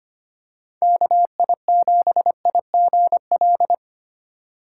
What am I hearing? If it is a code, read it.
Morse code, heard as a keyed tone, about 25 wpm, 690 Hz. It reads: KI7IGL
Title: USA Call Signs - 6 Characters - 2x Character Spacing - 25wpm